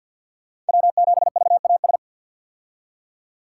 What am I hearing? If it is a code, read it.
Morse code, heard as a keyed tone, about 51 wpm, 700 Hz. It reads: W7VAS
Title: K1USN SST QSO Practice - 50wpm